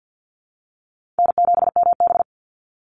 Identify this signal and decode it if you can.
Morse code, heard as a keyed tone, about 50 wpm, 700 Hz. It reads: N7MB
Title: Call Signs - 50wpm